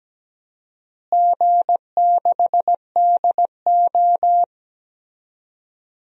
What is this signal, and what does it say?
Morse code, heard as a keyed tone, about 17 wpm, 700 Hz. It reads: G6DO
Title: DXCC-100 Call Signs - 4 Characters - Repeated 3x - 17wpm